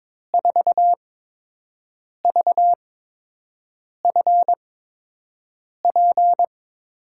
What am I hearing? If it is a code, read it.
Morse code, heard as a keyed tone, about 22 wpm, spaced as for 6 wpm, 700 Hz. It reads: 4VFP
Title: USA Call Signs - 6 Characters - 8x Character Spacing - 22wpm